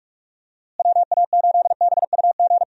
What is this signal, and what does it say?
Morse code, heard as a keyed tone, about 44 wpm, 700 Hz. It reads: WA8BUG
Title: USA Call Signs - 6 Characters - Repeated 2x - 45wpm